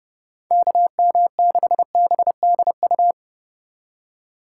Morse code, heard as a keyed tone, about 29 wpm, 700 Hz. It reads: KM6BDU